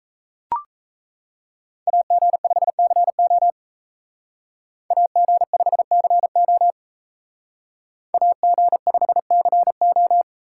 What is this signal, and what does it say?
Morse code, heard as a keyed tone, about 37 wpm, 700 Hz. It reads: AG5CO AG5CO AG5CO